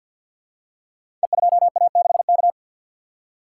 Morse code, heard as a keyed tone, about 51 wpm, 700 Hz. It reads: E2A6K
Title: Code Groups - 5 Letter:Numbers - 50wpm